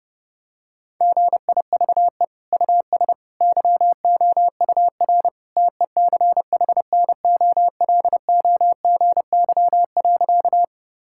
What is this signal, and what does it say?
Morse code, heard as a keyed tone, about 30 wpm, 700 Hz. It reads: GIVE US YOUR TECHNOLOGY.
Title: Sentences from Top 1000 Words - Review Part 5 - 30wpm